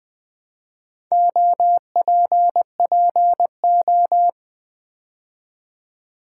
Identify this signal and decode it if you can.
Morse code, heard as a keyed tone, about 20 wpm, 700 Hz. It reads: OPPO
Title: Build A Word - 2 Letters At A Time - 6-Letter Words - 20wpm